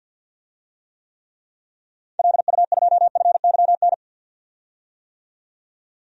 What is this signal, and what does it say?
Morse code, heard as a keyed tone, about 52 wpm, 700 Hz. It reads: LU2FYN
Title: DXCC-100 Call Signs - 6 Characters - 50wpm